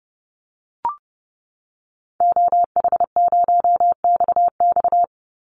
Morse code, heard as a keyed tone, about 30 wpm, 700 Hz. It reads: OH0XX
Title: Call Signs Repeated 3x - 30wpm